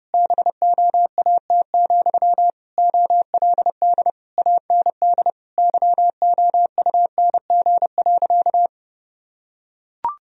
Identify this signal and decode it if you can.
Morse code, heard as a keyed tone, about 32 wpm, 700 Hz. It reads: BOAT, OLD AND YOUNG.
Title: Sentences from Top 1000 Words - Review Part 3 - 30wpm